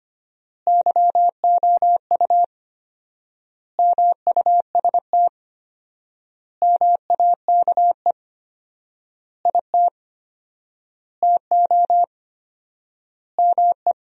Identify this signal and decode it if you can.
Morse code, heard as a keyed tone, about 25 wpm, 700 Hz. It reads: YOU MUST MAKE IT TO ME
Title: 4x Word-Spacing - Sentences from Top 500 Words - 25wpm